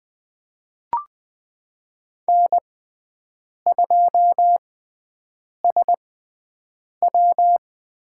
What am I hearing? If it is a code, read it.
Morse code, heard as a keyed tone, about 21 wpm, spaced as for 7 wpm, 700 Hz. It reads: N2SW
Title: USA Call Signs - 6x Character Spacing - 20wpm